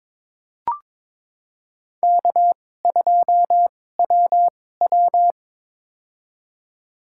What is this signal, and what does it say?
Morse code, heard as a keyed tone, about 22 wpm, spaced as for 16 wpm, 700 Hz. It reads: K2WW
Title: USA Call Signs - 4 Characters - 2x Character Spacing - 22wpm